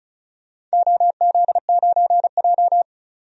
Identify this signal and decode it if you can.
Morse code, heard as a keyed tone, about 36 wpm, 700 Hz. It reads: OZ9J
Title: DXCC-100 Call Signs - 4 Characters - 35wpm